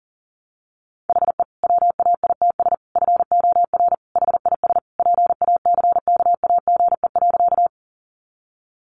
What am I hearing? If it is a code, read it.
Morse code, heard as a keyed tone, about 40 wpm, 700 Hz. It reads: HE WAITS FOR HIS PACKAGE.
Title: Additional Easy Phrases 40wpm